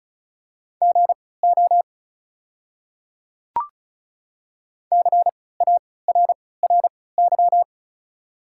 Morse code, heard as a keyed tone, about 34 wpm, spaced as for 20 wpm, 690 Hz. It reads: GO CARRY
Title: 3x Character Spacing - Top 500 Words - 35wpm